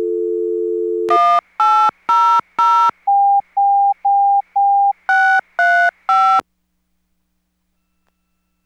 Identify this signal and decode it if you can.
DTMF, heard as telephone touch tones, tones of about 330 ms. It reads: 1800 635